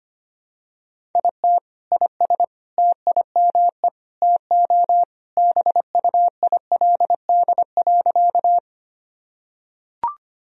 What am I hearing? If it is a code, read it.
Morse code, heard as a keyed tone, about 25 wpm, 700 Hz. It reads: IT IS TIME TO BUILD.
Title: Sentences from Top 1000 Words - Review Part 1 - 25wpm